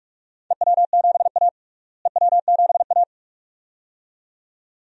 Morse code, heard as a keyed tone, about 47 wpm, 700 Hz. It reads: EW7A EW7A
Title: DXCC-100 Call Signs - 4 Characters - Repeated 2x - 45wpm